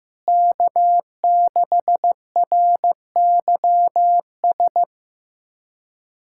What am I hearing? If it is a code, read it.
Morse code, heard as a keyed tone, about 16 wpm, 700 Hz. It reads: K6RYS
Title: USA Callsign DE Callsign - 5 Characters - 15wpm